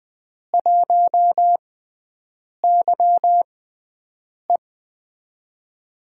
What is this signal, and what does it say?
Morse code, heard as a keyed tone, about 20 wpm, spaced as for 7 wpm, 700 Hz. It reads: 1YE
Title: USA Call Signs - 4 Characters - 6x Character Spacing - 20wpm